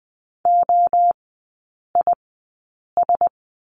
Morse code, heard as a keyed tone, about 20 wpm, spaced as for 8 wpm, 700 Hz. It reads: OIS
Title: 138 Lesson 23e - Letter f - Three Characters 20wpm